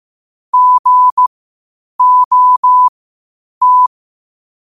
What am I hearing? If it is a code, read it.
Morse code, heard as a keyed tone, about 15 wpm, spaced as for 8 wpm, 1000 Hz. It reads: GOT
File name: qotdfiles / 8wpm